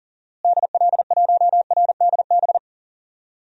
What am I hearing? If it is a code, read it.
Morse code, heard as a keyed tone, about 40 wpm, 700 Hz. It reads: DL1RDB
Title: DXCC-100 Call Signs - 6 Characters - Repeated 3x - 40wpm